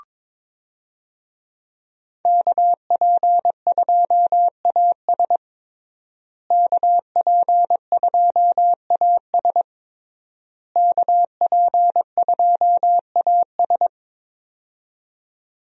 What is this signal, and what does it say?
Morse code, heard as a keyed tone, about 22 wpm, 700 Hz. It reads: KP2AS KP2AS KP2AS
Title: USA Call Signs - 5 Characters - Repeated 3x - 22wpm